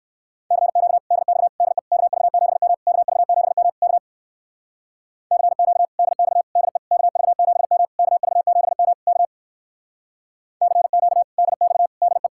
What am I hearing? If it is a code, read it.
Morse code, heard as a keyed tone, about 58 wpm, 700 Hz. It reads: CQ DX DE KV7M KV7M K CQ DX DE KV7M KV7M K CQ DX DE